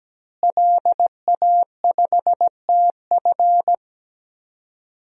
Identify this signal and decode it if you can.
Morse code, heard as a keyed tone, about 17 wpm, 700 Hz. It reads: LA5TF
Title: DXCC-100 Call Signs - 5 Characters - Repeated 2x - 17wpm